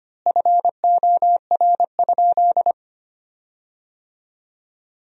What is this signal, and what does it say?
Morse code, heard as a keyed tone, about 25 wpm, 700 Hz. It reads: FOR?
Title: 4x Word-Spacing - Sentences from Top 100 Words - 25wpm